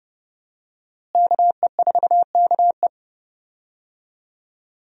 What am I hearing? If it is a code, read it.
Morse code, heard as a keyed tone, about 30 wpm, 700 Hz. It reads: KE4KE
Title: USA Call Signs - 5 Characters - Repeated 2x - 30wpm